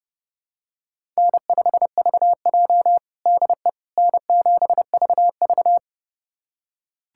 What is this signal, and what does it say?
Morse code, heard as a keyed tone, about 30 wpm, 700 Hz. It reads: N5VJ DE N7VV
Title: USA Callsign DE Callsign - 4 Characters - 30wpm